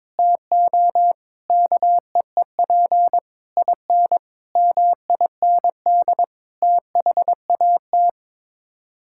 Morse code, heard as a keyed tone, about 21 wpm, 700 Hz. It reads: TO KEEP IN MIND THAT
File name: practice / 22wpm / Sets of 8 Words - 22wpm